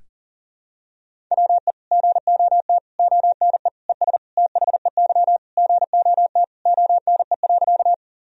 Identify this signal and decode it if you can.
Morse code, heard as a keyed tone, about 40 wpm, 700 Hz. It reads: WE GOT ONE ES THEY GOT ONE.
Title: Sentences from Top 300 Words and Abbreviations 40wpm